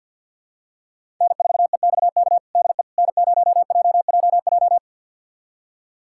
Morse code, heard as a keyed tone, about 50 wpm, 700 Hz. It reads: N4EXK DE N0JJJ
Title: USA Callsign DE Callsign - 5 Characters - 50wpm